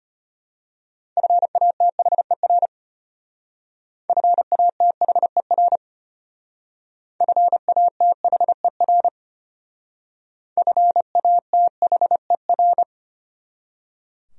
Morse code, heard as a keyed tone, about 31 wpm, 700 Hz. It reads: FATHER FATHER FATHER FATHER
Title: Speed-Racing Top 300 Words 25wpm